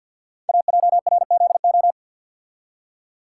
Morse code, heard as a keyed tone, about 51 wpm, 700 Hz. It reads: AJRZO
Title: Code Groups - 5 Letter:Numbers - 50wpm